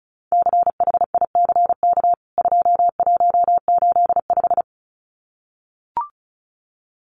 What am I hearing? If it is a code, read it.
Morse code, heard as a keyed tone, about 34 wpm, 700 Hz. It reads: CHICK 2185
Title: CWT Exchange 35wpm